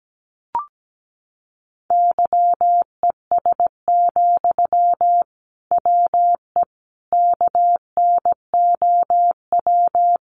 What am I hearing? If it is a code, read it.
Morse code, heard as a keyed tone, about 17 wpm, 700 Hz. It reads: YES, WE KNOW
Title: Sentences from Top 500 Words 17wpm